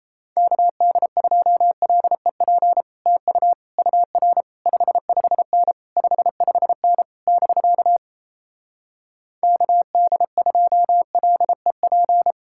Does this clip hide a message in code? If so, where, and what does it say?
Yes: Morse code from 0.4 to 12.3 s, heard as a keyed tone, about 37 wpm, 700 Hz. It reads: KD2LEP TU UR 55N 55N <BK> KD2LEP